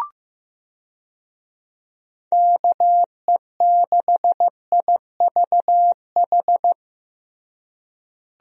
Morse code, heard as a keyed tone, about 15 wpm, 700 Hz. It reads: KE6IVH
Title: USA Call Signs - 6 Characters - Repeated 2x - 15wpm